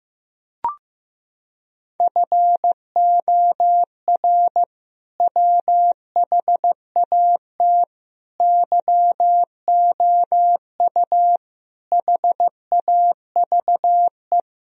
Morse code, heard as a keyed tone, about 15 wpm, 700 Hz. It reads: FOR WHAT YOU HAVE
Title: Sentences from Top 1000 Words - Review Part 2 - 15wpm